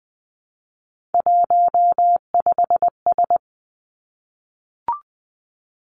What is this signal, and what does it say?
Morse code, heard as a keyed tone, about 19 wpm, 700 Hz. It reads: 15S
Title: Three Challenging Letter-Number Combinations 20wpm